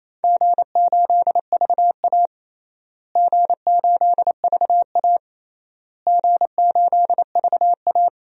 Morse code, heard as a keyed tone, about 28 wpm, 700 Hz. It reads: G8VA G8VA G8VA
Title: DXCC-100 Call Signs - 4 Characters - Repeated 3x - 28wpm